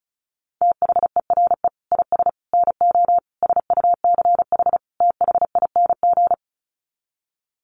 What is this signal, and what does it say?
Morse code, heard as a keyed tone, about 34 wpm, 700 Hz. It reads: THERE IS NO SUCH THING
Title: Sets of 5 Words Encore 35wpm